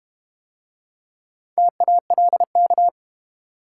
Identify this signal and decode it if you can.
Morse code, heard as a keyed tone, about 31 wpm, 700 Hz. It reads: TALK
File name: Easy Words - Speed-Racing with Hints - 25wpm